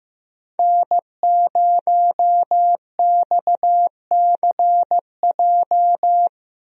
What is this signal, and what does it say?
Morse code, heard as a keyed tone, about 15 wpm, 700 Hz. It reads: N0XCJ
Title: USA Call Signs - 5 Characters - Repeated 3x - 15wpm